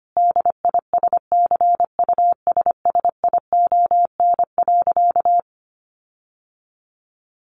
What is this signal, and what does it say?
Morse code, heard as a keyed tone, about 25 wpm, 700 Hz. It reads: DISCUSSION.
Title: Sentences from Top 1500 Words 25wpm